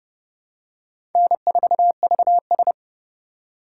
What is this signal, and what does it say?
Morse code, heard as a keyed tone, about 30 wpm, 700 Hz. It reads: N4VS
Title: USA Call Signs - 4 Characters - 2x Character Spacing - 30wpm